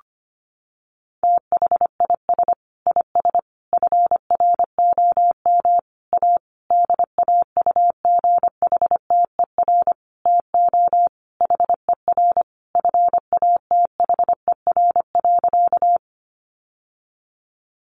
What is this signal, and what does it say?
Morse code, heard as a keyed tone, about 25 wpm, 700 Hz. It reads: THIS IS FROM A DAUGHTER TO HER FATHER.
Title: Sentences from Top 1500 Words 25wpm